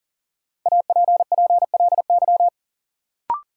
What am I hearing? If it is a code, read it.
Morse code, heard as a keyed tone, about 40 wpm, 700 Hz. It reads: APPLY
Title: Sentences from Top 1200 Words - New Word Drill - 40wpm